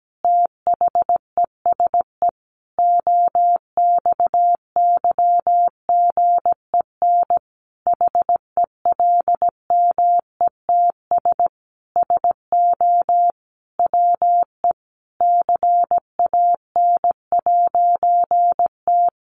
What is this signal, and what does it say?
Morse code, heard as a keyed tone, about 17 wpm, 700 Hz. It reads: THESE OXYGEN HELMETS SO WE CAN'T
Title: A Sound Of Thunder 17wpm